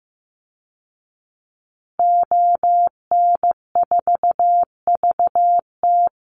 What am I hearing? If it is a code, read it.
Morse code, heard as a keyed tone, about 15 wpm, 700 Hz. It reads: ON4VT
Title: Call Signs Repeated 3x - 15wpm